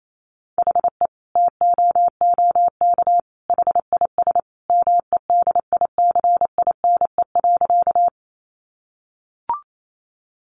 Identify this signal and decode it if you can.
Morse code, heard as a keyed tone, about 28 wpm, 700 Hz. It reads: HE TOOK HIS MEDICINE.
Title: Sentences from Top 900 Words 28wpm